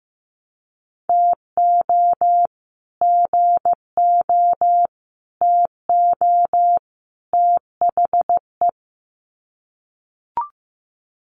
Morse code, heard as a keyed tone, about 15 wpm, 700 Hz. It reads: TO GO TO THE
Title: Sets of 4 Words 15wpm